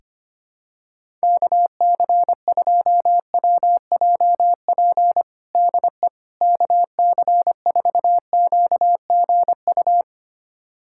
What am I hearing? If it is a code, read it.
Morse code, heard as a keyed tone, about 25 wpm, 700 Hz. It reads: KC2WJP DE KC4QGU
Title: USA Callsign DE Callsign - 6 Characters - 25wpm